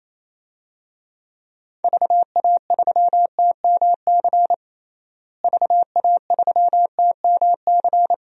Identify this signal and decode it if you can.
Morse code, heard as a keyed tone, about 28 wpm, 700 Hz. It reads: VA3TMC VA3TMC
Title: DXCC-100 Call Signs - 6 Characters - Repeated 2x - 28wpm